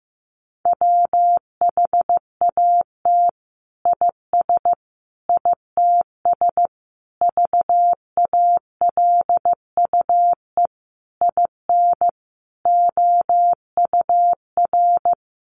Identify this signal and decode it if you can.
Morse code, heard as a keyed tone, about 15 wpm, 700 Hz. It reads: WHAT IS ITS VALUE IN OUR
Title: Sentences from Top 600 Words 15wpm